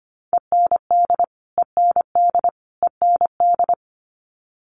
Morse code, heard as a keyed tone, about 25 wpm, 700 Hz. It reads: END END END
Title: Top 200 Words Repeated 25wpm